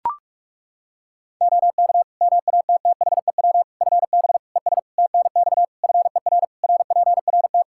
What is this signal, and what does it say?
Morse code, heard as a keyed tone, about 41 wpm, 700 Hz. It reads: OK MATTHEW FB ES TNX FER RPRT
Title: Practice QSO - Exchange 2 - Info - 45wpm